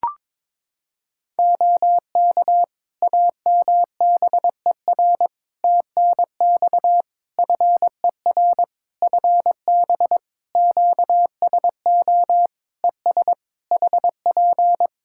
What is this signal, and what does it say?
Morse code, heard as a keyed tone, about 22 wpm, 700 Hz. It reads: OK AMBER TNX FER FB QSO ES HP
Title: Practice QSO - Exchange 4 - Ending - 22wpm